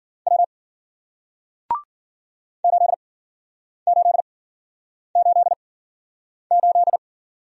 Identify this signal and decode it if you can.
Morse code, heard as a keyed tone, about 52 wpm, spaced as for 9 wpm, 700 Hz. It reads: W 8888